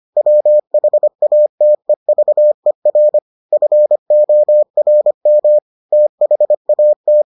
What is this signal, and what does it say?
Morse code, heard as a keyed tone, about 25 wpm, 600 Hz. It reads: WHATEVER FORM THAT